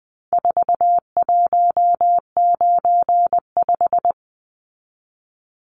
Morse code, heard as a keyed tone, about 20 wpm, 700 Hz. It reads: 4195
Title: Numbers - 4-Digit 20wpm